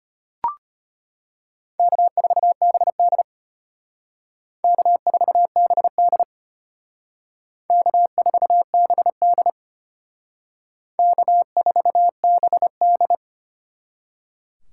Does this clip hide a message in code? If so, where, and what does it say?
Yes: Morse code from 1.8 to 13.2 s, heard as a keyed tone, about 32 wpm, 700 Hz. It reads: K4BD K4BD K4BD K4BD